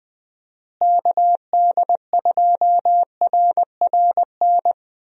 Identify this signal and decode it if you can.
Morse code, heard as a keyed tone, about 20 wpm, 700 Hz. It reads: KD2RRN
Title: USA Call Signs - 6 Characters - Repeated 3x - 20wpm